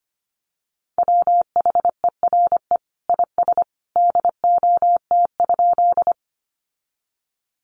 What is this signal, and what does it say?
Morse code, heard as a keyed tone, about 25 wpm, 700 Hz. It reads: WHERE IS DOT?
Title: Sentences from Top 1500 Words 25wpm